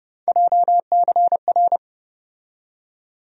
Morse code, heard as a keyed tone, about 30 wpm, 700 Hz. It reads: JCR
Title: USA Call Sign Postfixes - 3 Letters 30wpm